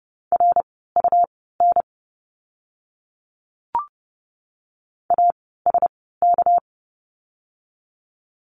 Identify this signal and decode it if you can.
Morse code, heard as a keyed tone, about 30 wpm, spaced as for 17 wpm, 700 Hz. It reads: RUN ASK
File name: course / 30:10wpm / 237 Lesson 38f - Letter q - Words 30wpm